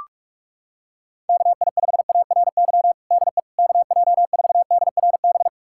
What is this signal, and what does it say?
Morse code, heard as a keyed tone, about 42 wpm, 700 Hz. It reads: KI5ARY DE KJ4DRB
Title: USA Callsign DE Callsign 45wpm